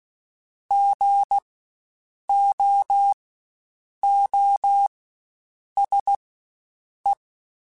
Morse code, heard as a keyed tone, about 16 wpm, spaced as for 7 wpm, 800 Hz. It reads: GOOSE
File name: Spaced Alphas / Spaced Alphas mp3s / Gs sp @15